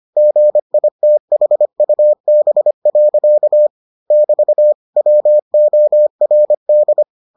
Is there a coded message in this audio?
Yes: Morse code from 0.2 to 7.0 s, heard as a keyed tone, about 25 wpm, 600 Hz. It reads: GITHUB. <BT> WORD